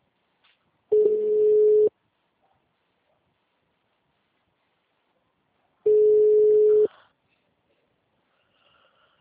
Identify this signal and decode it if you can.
Ringback tone (Europe, 425 Hz).
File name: out- -2015- - - .13132